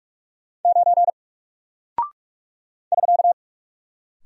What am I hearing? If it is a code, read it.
Morse code, heard as a keyed tone, about 45 wpm, 700 Hz. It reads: 9 <SK>
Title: Single Character (Letter:Number:Punctuation:Prosign) Alt 45wpm